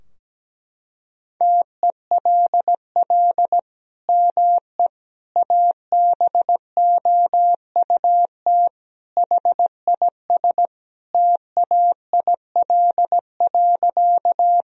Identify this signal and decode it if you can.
Morse code, heard as a keyed tone, about 17 wpm, 700 Hz. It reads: TELL ME ABOUT HIS TAIL.
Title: Sentences from Top 1000 Words - Review Part 3 - 17wpm